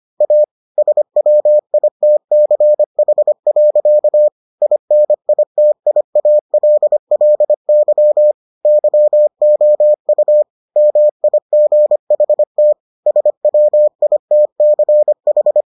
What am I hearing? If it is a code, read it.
Morse code, heard as a keyed tone, about 25 wpm, 600 Hz. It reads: A SWITCH. INITIALLY YOU MIGHT SWITCH